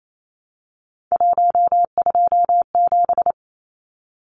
Morse code, heard as a keyed tone, about 28 wpm, 700 Hz. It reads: 127
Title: Numbers - 3-Digit 28wpm